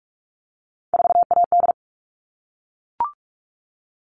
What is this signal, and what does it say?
Morse code, heard as a keyed tone, about 44 wpm, 700 Hz. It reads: 4AD
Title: All Cut Numbers - 3-Digit 45wpm